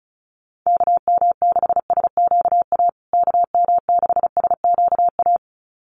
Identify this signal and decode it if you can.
Morse code, heard as a keyed tone, about 34 wpm, 700 Hz. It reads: KM6SQA KM6SQA
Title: USA Call Signs Repeated 2x 35wpm